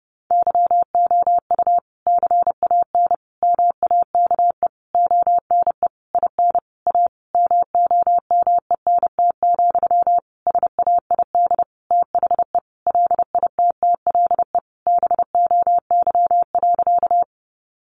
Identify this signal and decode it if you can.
Morse code, heard as a keyed tone, about 28 wpm, 700 Hz. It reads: YOU CAN MAKE ONE IN A MOMENT, SAID THE LITTLE BOY.
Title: Sentences from Top 1000 Words 30wpm